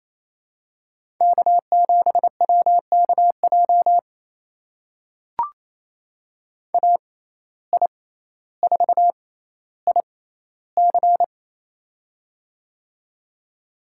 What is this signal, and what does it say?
Morse code, heard as a keyed tone, about 28 wpm, spaced as for 14 wpm, 700 Hz. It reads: K7WKJ AI4IC
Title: USA Call Signs - 5 Characters - 6x Character Spacing - 28wpm